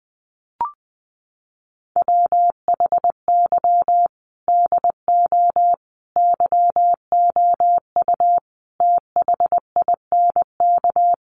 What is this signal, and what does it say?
Morse code, heard as a keyed tone, about 20 wpm, 700 Hz. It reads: WHY DO YOU THINK
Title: Sets of 5 Words Encore2 20wpm